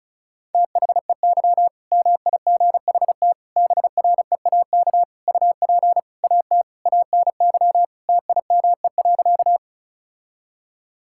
Morse code, heard as a keyed tone, about 34 wpm, 690 Hz. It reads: THEY MIGHT BREAK UP AT ANY TIME.
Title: Sentences from Top 1000 Words - Review Part 6 - 35wpm